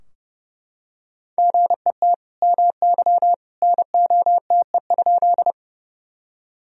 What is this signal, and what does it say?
Morse code, heard as a keyed tone, about 32 wpm, 700 Hz. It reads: GET MY NOTE?
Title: Sentences from Top 1000 Words - Review Part 2 - 30wpm